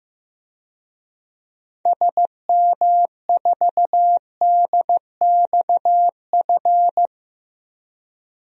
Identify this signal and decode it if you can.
Morse code, heard as a keyed tone, about 15 wpm, 700 Hz. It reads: SM4DXF
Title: DXCC-100 Call Signs - 6 Characters - Repeated 2x - 15wpm